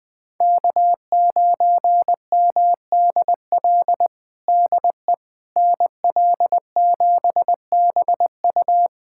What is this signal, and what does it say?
Morse code, heard as a keyed tone, about 20 wpm, 700 Hz. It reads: K9MDL DE NL7BU
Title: USA Callsign DE Callsign - 5 Characters - 20wpm